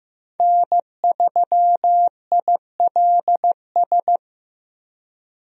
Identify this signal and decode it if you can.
Morse code, heard as a keyed tone, about 15 wpm, 700 Hz. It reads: N3ILS
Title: USA Call Signs - 5 Characters - Repeated 2x - 15wpm